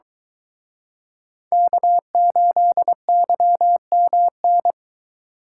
Morse code, heard as a keyed tone, about 23 wpm, 700 Hz. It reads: K8YMN